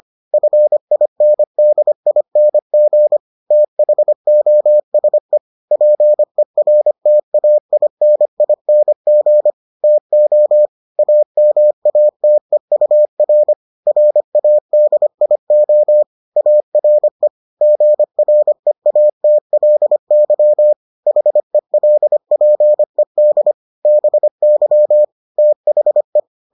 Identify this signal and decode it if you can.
Morse code, heard as a keyed tone, about 25 wpm, 600 Hz. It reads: FINDING THOSE PERTAINING TO AMATEUR RADIO ARE GREATLY HELPED BY THE